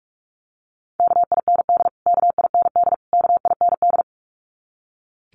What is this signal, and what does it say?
Morse code, heard as a keyed tone, about 44 wpm, 700 Hz. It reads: KIND KIND KIND
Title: Top 500 Words Repeated 45wpm